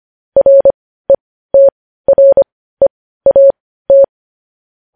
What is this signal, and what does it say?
Morse code, heard as a keyed tone, about 25 wpm, spaced as for 15 wpm, 550 Hz. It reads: RETREAT
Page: news headlines in morse code at 15 wpm